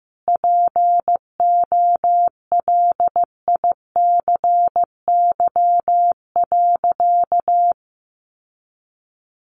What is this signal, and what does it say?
Morse code, heard as a keyed tone, about 15 wpm, 700 Hz. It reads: POLICY.
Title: Sentences from Top 1400 Words 15wpm